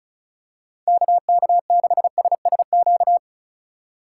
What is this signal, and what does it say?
Morse code, heard as a keyed tone, about 36 wpm, 700 Hz. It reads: KK6SSQ
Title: USA Call Signs - 4x Character Spacing - 35wpm